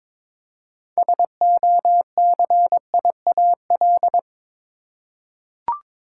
Morse code, heard as a keyed tone, about 21 wpm, 700 Hz. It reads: SOCIAL
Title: Practice QSO - Exchange 3 - More Info - 22wpm